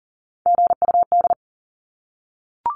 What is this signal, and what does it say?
Morse code, heard as a keyed tone, about 40 wpm, 700 Hz. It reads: GUD
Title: Abbreviations 40wpm